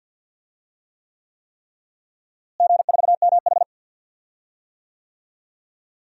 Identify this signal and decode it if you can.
Morse code, heard as a keyed tone, about 51 wpm, 700 Hz. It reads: G4MH
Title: DXCC-100 Call Signs - 4 Characters - Repeated 3x - 50wpm